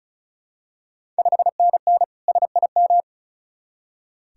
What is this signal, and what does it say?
Morse code, heard as a keyed tone, about 36 wpm, 700 Hz. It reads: 5NN SIM
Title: Ontario QSO Party Exchange 35wpm